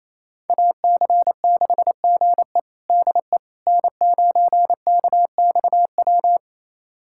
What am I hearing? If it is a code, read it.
Morse code, heard as a keyed tone, about 27 wpm, 700 Hz. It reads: AC6GE DE N9KXW
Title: USA Callsign DE Callsign - 5 Characters - 28wpm